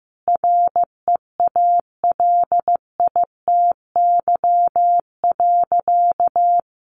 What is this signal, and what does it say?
Morse code, heard as a keyed tone, about 15 wpm, 700 Hz. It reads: REALITY.
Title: Sentences from Top 1200 Words 15wpm